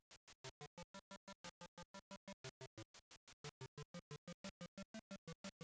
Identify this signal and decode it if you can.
There is no code.